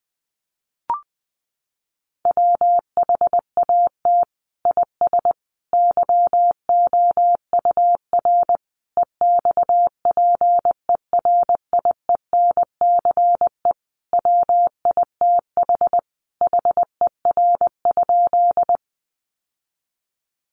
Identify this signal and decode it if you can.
Morse code, heard as a keyed tone, about 20 wpm, 700 Hz. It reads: WHAT IS YOUR EXPERIENCE WITH HER?
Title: Sentences from Top 600 Words 20wpm